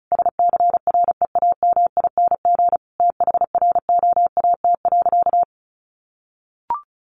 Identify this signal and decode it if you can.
Morse code, heard as a keyed tone, about 34 wpm, 700 Hz. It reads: SCREAMING THROAT.
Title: A Sound Of Thunder 35wpm